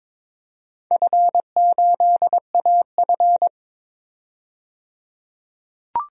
Morse code, heard as a keyed tone, about 22 wpm, 700 Hz. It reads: F8AF
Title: DXCC-100 Call Signs - 4 Characters - 22wpm